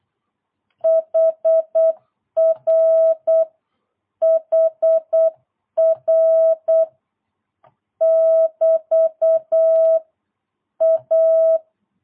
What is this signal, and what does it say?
Morse code, heard as a keyed tone, about 8 wpm, 650 Hz. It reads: HR HR <BT> A
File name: jh8cbh / 2022cw